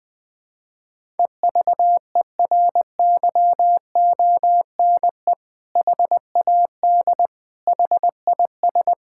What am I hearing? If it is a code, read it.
Morse code, heard as a keyed tone, about 20 wpm, 700 Hz. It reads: EVERYONE HAD HIS
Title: Sentences from Top 1000 Words - Review Part 6 - 20wpm